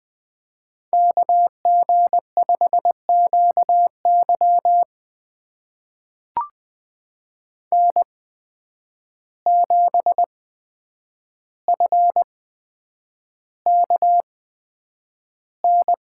Morse code, heard as a keyed tone, about 20 wpm, spaced as for 9 wpm, 700 Hz. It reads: KG5QY N7FKN
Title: USA Call Signs - 5 Characters - 8x Character Spacing - 20wpm